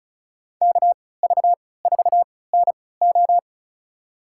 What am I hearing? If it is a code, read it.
Morse code, heard as a keyed tone, about 34 wpm, spaced as for 20 wpm, 700 Hz. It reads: KV4NO
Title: USA Call Signs - 5 Characters - 3x Character Spacing - 35wpm